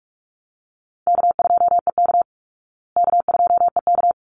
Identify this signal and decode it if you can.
Morse code, heard as a keyed tone, about 44 wpm, 700 Hz. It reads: K2EK K2EK
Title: Easy Call Signs Repeated 2x - 45wpm